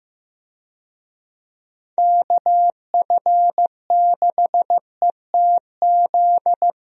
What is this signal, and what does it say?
Morse code, heard as a keyed tone, about 15 wpm, 700 Hz. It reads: KF6ETZ
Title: USA Call Signs - 6 Characters - Repeated 3x - 15wpm